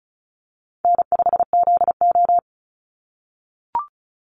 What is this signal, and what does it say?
Morse code, heard as a keyed tone, about 34 wpm, 700 Hz. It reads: N5ZO
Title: Call Signs - 35wpm